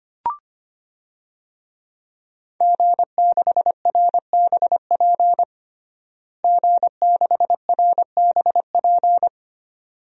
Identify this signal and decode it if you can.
Morse code, heard as a keyed tone, about 25 wpm, 700 Hz. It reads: G6RBP G6RBP
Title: DXCC-100 Call Signs - 5 Characters - Repeated 2x - 25wpm